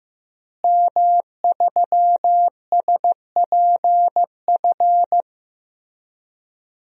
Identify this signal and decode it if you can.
Morse code, heard as a keyed tone, about 15 wpm, 700 Hz. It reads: M3SPF
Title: DXCC-100 Call Signs - 5 Characters - Repeated 3x - 15wpm